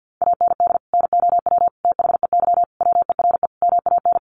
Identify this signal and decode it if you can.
Morse code, heard as a keyed tone, about 50 wpm, 700 Hz. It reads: AND NOW THEY WERE MAN
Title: Sentences from Top 1100 Words 50wpm